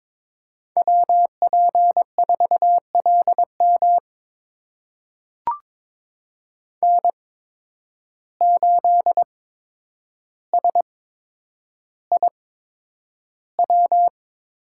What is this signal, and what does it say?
Morse code, heard as a keyed tone, about 22 wpm, spaced as for 9 wpm, 700 Hz. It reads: WP4LM N8SIW
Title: USA Call Signs - 5 Characters - 8x Character Spacing - 22wpm